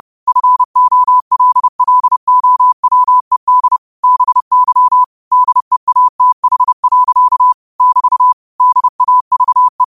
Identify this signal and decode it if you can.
Morse code, heard as a keyed tone, about 30 wpm, 1000 Hz. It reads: FOLLOWED BY DEATH. <BT> DAVE
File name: qotdfiles / 30wpm